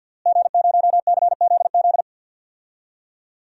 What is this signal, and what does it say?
Morse code, heard as a keyed tone, about 48 wpm, 700 Hz. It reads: G0CZZ
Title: DXCC-100 Call Signs - 5 Characters - Repeated 3x - 50wpm